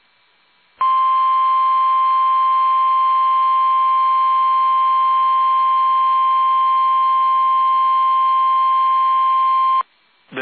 Emergency Alert System (SAME) tones (the 1050 Hz attention tone). It is heard from 0.8 to 9.3 s.
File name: eas / KUSO / KUSO